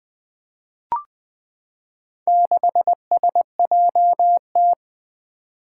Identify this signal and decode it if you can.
Morse code, heard as a keyed tone, about 20 wpm, 700 Hz. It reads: 6SJT
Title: Four Challenging Letter-Number Combinations 20wpm